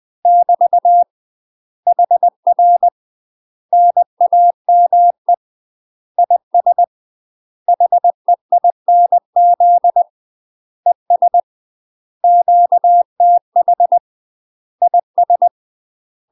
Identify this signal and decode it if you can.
Morse code, heard as a keyed tone, about 20 wpm, 700 Hz. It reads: <BT> HR NAME IS HEINZ ES QTH IS